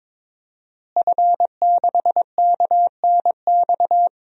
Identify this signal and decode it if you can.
Morse code, heard as a keyed tone, about 22 wpm, 700 Hz. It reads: F6KNX
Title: DXCC-100 Call Signs - 5 Characters - Repeated 3x - 22wpm